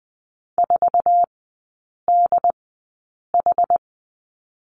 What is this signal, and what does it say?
Morse code, heard as a keyed tone, about 20 wpm, spaced as for 8 wpm, 700 Hz. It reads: 4DH
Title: 067 Lesson 12e - Letter d - Three Characters 20wpm